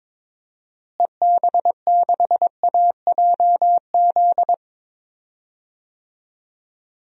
Morse code, heard as a keyed tone, about 22 wpm, 700 Hz. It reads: EB6AJZ
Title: DXCC-100 Call Signs - 6 Characters - Repeated 3x - 22wpm